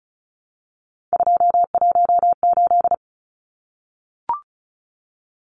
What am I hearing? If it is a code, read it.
Morse code, heard as a keyed tone, about 34 wpm, 700 Hz. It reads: 218
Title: Numbers - 3-Digit 35wpm